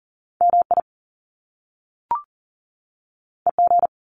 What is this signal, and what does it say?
Morse code, heard as a keyed tone, about 40 wpm, 700 Hz. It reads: MI EG
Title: Top Two Letter Combinations in Words 40wpm